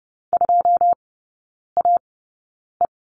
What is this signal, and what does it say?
Morse code, heard as a keyed tone, about 30 wpm, spaced as for 9 wpm, 700 Hz. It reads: 2AE
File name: 081 Lesson 14e - Number 2 - Three Characters 30wpm